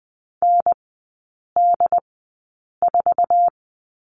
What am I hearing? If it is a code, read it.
Morse code, heard as a keyed tone, about 20 wpm, spaced as for 8 wpm, 700 Hz. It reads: ND4
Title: 067 Lesson 12e - Letter d - Three Characters 20wpm